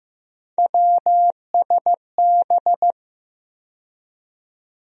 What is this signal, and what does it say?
Morse code, heard as a keyed tone, about 15 wpm, 700 Hz. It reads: WSB
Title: USA Call Sign Postfixes - 3 Letters 15wpm